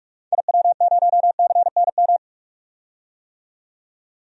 Morse code, heard as a keyed tone, about 41 wpm, 700 Hz. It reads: IW0CNM
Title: DXCC-100 Call Signs - 6 Characters - 45wpm